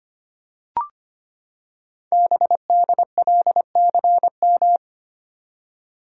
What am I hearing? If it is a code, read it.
Morse code, heard as a keyed tone, about 25 wpm, 700 Hz. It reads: BDLCM
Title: Code Groups - 5 Letters - 25wpm